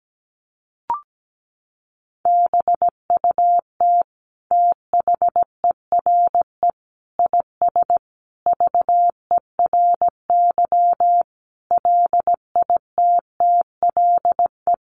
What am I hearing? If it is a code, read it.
Morse code, heard as a keyed tone, about 17 wpm, 700 Hz. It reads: BUT THERE IS VERY LITTLE
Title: Sentences from Top 1000 Words 17wpm